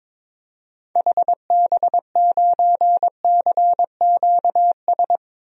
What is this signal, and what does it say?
Morse code, heard as a keyed tone, about 21 wpm, 700 Hz. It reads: HB9CQS
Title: DXCC-100 Call Signs - 6 Characters - 22wpm